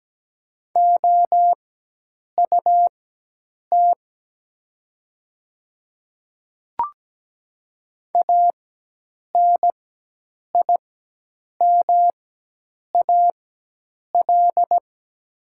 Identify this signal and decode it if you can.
Morse code, heard as a keyed tone, about 17 wpm, spaced as for 8 wpm, 700 Hz. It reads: OUT ANIMAL
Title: 4x Character Spacing - Top 200 Words - 17wpm